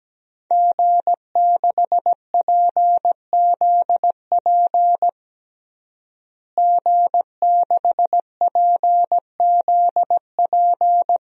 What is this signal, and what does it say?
Morse code, heard as a keyed tone, about 17 wpm, 700 Hz. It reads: G6PZP G6PZP
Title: DXCC-100 Call Signs - 5 Characters - Repeated 2x - 17wpm